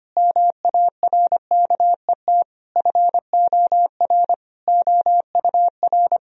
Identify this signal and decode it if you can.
Morse code, heard as a keyed tone, about 25 wpm, 700 Hz. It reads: MARKET FOR OUR
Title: Sentences from Top 1000 Words - Review Part 3 - 25wpm